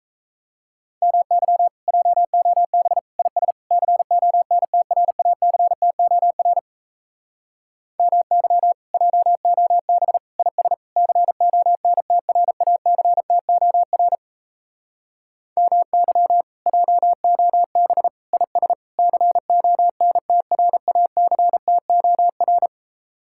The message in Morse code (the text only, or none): MY JOB IS CONTRACTOR MY JOB IS CONTRACTOR MY JOB IS CONTRACTOR